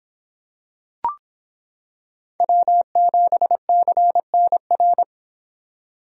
Morse code, heard as a keyed tone, about 25 wpm, 700 Hz. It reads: W7CNR